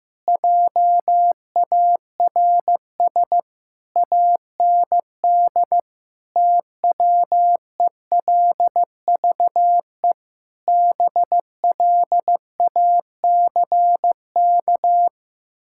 Morse code, heard as a keyed tone, about 15 wpm, 700 Hz. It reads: JARS AND TWELVE BLACK
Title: Pangram Challenge v2 - 15wpm